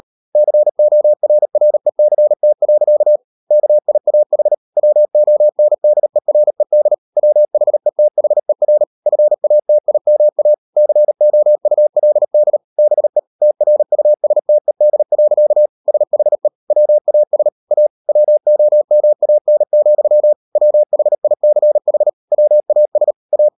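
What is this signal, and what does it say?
Morse code, heard as a keyed tone, about 38 wpm, 600 Hz. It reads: CORRECT. KIAH WONDERED WHETHER FATIMA COULD BE TRUSTED. SHE WAS A WOMAN, WHICH WAS A
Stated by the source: Morse code practice at 38 WPM, sending Ken Follett's novel Never in English, from CW med Gnister.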